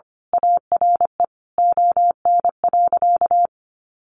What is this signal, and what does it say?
Morse code, heard as a keyed tone, about 25 wpm, 700 Hz. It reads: ARE ON.
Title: Sentences from Top 1100 Words 25wpm